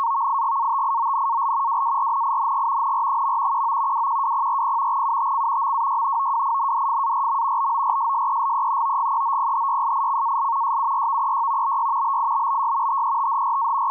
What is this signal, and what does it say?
Two-tone teleprinter signal, about 170 Hz shift.
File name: PhosphorTEX / navtex 518khz 8min